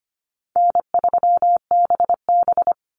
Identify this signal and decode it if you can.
Morse code, heard as a keyed tone, about 25 wpm, 700 Hz. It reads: N3BB
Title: Easy Call Signs Repeated 3x - 25wpm